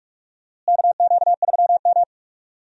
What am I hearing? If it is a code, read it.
Morse code, heard as a keyed tone, about 45 wpm, 700 Hz. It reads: KQ3M
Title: USA Call Signs - 4 Characters - Repeated 3x - 45wpm